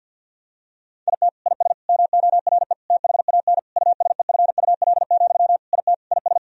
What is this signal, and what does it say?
Morse code, heard as a keyed tone, about 50 wpm, 700 Hz. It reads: IT IS MORE THAN USEFUL, IT IS
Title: Sentences from Top 1000 Words - Review Part 4 - 50wpm